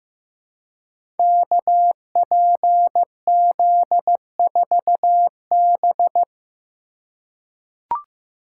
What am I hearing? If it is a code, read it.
Morse code, heard as a keyed tone, about 15 wpm, 700 Hz. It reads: KPZ4B
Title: Code Groups - 5 Letter:Numbers - 15wpm